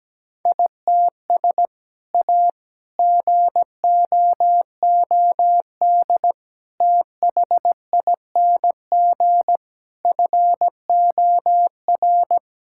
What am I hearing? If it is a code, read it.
Morse code, heard as a keyed tone, about 17 wpm, 700 Hz. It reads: ITS A GOOD THING FOR
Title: Sentences from Top 1000 Words - Review Part 5 - 17wpm